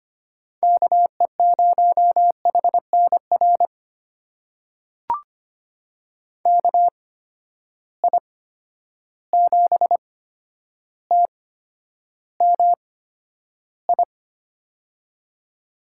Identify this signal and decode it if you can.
Morse code, heard as a keyed tone, about 25 wpm, spaced as for 11 wpm, 700 Hz. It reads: KE0HNR KI7TMI
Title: USA Call Signs - 6 Characters - 8x Character Spacing - 25wpm